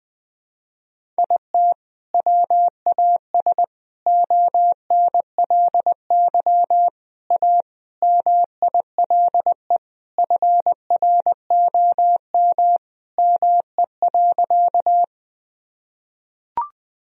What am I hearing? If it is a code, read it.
Morse code, heard as a keyed tone, about 20 wpm, 700 Hz. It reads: IT WAS ONLY A MILE FROM ME.
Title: Sentences from Top 1000 Words - Review Part 1 - 20wpm